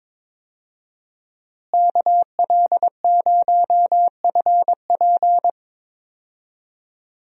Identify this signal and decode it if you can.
Morse code, heard as a keyed tone, about 21 wpm, 700 Hz. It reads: KL0FP
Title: USA Call Signs - 5 Characters - Repeated 2x - 22wpm